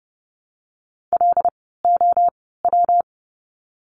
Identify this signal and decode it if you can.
Morse code, heard as a keyed tone, about 30 wpm, spaced as for 17 wpm, 700 Hz. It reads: LOW
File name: course / 30:10wpm / 115 Lesson 19f - Letter w - Words 30wpm